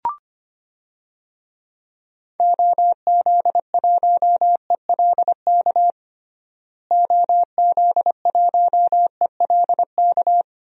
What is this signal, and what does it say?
Morse code, heard as a keyed tone, about 25 wpm, 700 Hz. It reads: OZ1ELK OZ1ELK
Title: DXCC-100 Call Signs - 6 Characters - Repeated 2x - 25wpm